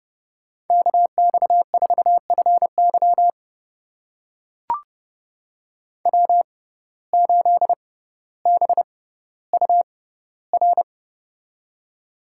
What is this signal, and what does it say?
Morse code, heard as a keyed tone, about 30 wpm, spaced as for 15 wpm, 700 Hz. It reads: KX4FY W8BUR
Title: USA Call Signs - 5 Characters - 6x Character Spacing - 30wpm